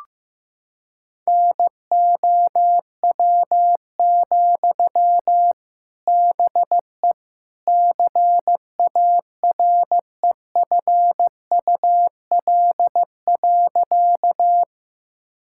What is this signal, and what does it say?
Morse code, heard as a keyed tone, about 15 wpm, 700 Hz. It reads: NOW, BE CAREFUL.
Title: Sentences from Top 1600 Words 15wpm